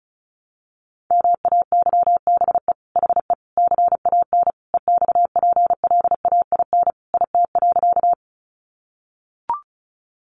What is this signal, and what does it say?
Morse code, heard as a keyed tone, about 34 wpm, 700 Hz. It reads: MAYBE HE CAN EXPLAIN IT.
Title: Sentences from Top 1100 Words 35wpm